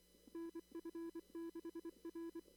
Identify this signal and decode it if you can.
Morse code, heard as a keyed tone, about 23 wpm, 330 Hz. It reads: NF6R
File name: CW / kt5j wpx cw-op-k5pi